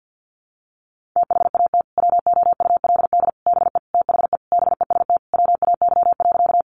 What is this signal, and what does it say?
Morse code, heard as a keyed tone, about 50 wpm, 700 Hz. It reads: THAT WOULD BE THE BEST WAY.
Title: Sentences from Top 400 Words 50wpm